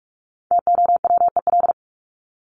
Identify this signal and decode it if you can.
Morse code, heard as a keyed tone, about 44 wpm, 700 Hz. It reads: TOWEL
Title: Top 1501-1750 Words 45wpm